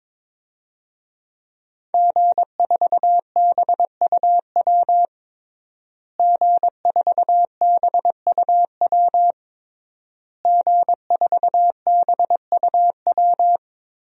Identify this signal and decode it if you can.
Morse code, heard as a keyed tone, about 22 wpm, 700 Hz. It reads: G4BUW G4BUW G4BUW
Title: DXCC-100 Call Signs - 5 Characters - Repeated 3x - 22wpm